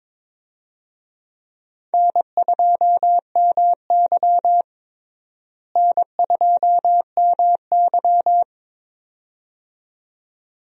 Morse code, heard as a keyed tone, about 22 wpm, 700 Hz. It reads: N2MY N2MY
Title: USA Call Signs - 4 Characters - Repeated 2x - 22wpm